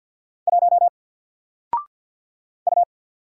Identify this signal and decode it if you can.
Morse code, heard as a keyed tone, about 46 wpm, 700 Hz. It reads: 1 U
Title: Single Letter-Number 50wpm